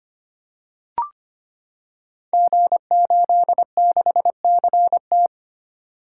Morse code, heard as a keyed tone, about 25 wpm, 700 Hz. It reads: G86CT
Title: Code Groups - 5 Letter:Numbers - 25wpm